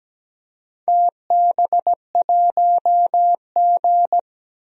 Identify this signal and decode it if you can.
Morse code, heard as a keyed tone, about 17 wpm, 700 Hz. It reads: TB1G
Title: DXCC-100 Call Signs - 4 Characters - 17wpm